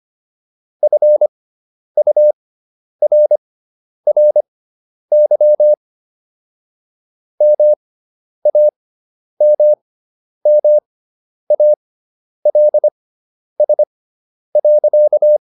Morse code, heard as a keyed tone, about 25 wpm, spaced as for 10 wpm, 600 Hz. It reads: FURRY MAMMALS.